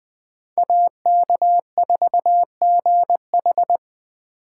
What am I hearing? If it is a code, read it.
Morse code, heard as a keyed tone, about 21 wpm, 700 Hz. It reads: AK4GH